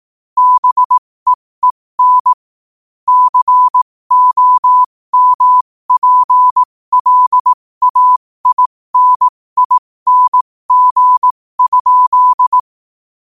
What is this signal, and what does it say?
Morse code, heard as a keyed tone, about 18 wpm, spaced as for 15 wpm, 1000 Hz. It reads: BEEN COMPLAINING?
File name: qotdfiles / 15wpm